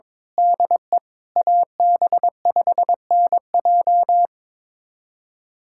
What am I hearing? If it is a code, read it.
Morse code, heard as a keyed tone, about 23 wpm, 700 Hz. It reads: DE AB5NJ
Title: USA Callsign DE Callsign - 5 Characters - 22wpm